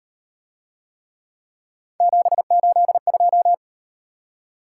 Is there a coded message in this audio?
Yes: Morse code from 2.0 to 3.5 s, heard as a keyed tone, about 38 wpm, 700 Hz. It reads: 782